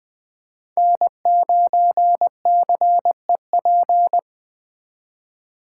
Morse code, heard as a keyed tone, about 20 wpm, 700 Hz. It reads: N9CEP